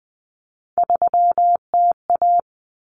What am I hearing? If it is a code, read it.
Morse code, heard as a keyed tone, about 20 wpm, 700 Hz. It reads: 3TA
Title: Common Cut Numbers - 3-Digit 20wpm